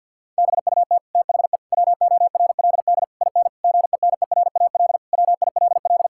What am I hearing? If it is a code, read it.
Morse code, heard as a keyed tone, about 50 wpm, 700 Hz. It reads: BUT THE WORLD IN GENERAL WILL
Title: Sentences from Top 1000 Words - Review Part 6 - 50wpm